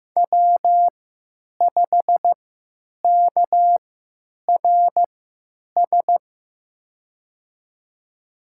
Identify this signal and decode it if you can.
Morse code, heard as a keyed tone, about 15 wpm, spaced as for 9 wpm, 700 Hz. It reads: W5KRS
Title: USA Call Signs - 5 Characters - 3x Character Spacing - 15wpm